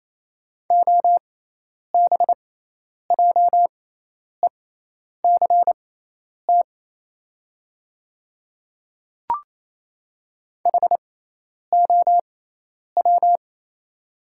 Morse code, heard as a keyed tone, about 28 wpm, spaced as for 10 wpm, 700 Hz. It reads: OBJECT HOW
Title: 6x Character Spacing - Top 500 Words - 28wpm